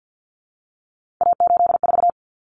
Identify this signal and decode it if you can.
Morse code, heard as a keyed tone, about 50 wpm, 700 Hz. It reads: A84
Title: Common Cut Numbers - 3-Digit 50wpm